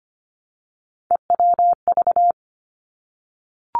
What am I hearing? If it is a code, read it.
Morse code, heard as a keyed tone, about 25 wpm, 700 Hz. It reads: EWV
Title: Three Challenging Letter-Number Combinations 25wpm